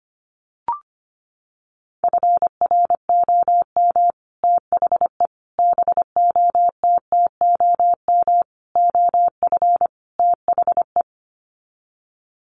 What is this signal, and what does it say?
Morse code, heard as a keyed tone, about 25 wpm, 700 Hz. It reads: FROM THE BOTTOM OF THE
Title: Sets of 5 Words Encore2 25wpm